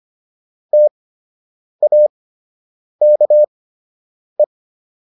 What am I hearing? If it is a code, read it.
Morse code, heard as a keyed tone, about 25 wpm, spaced as for 8 wpm, 600 Hz. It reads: TAKE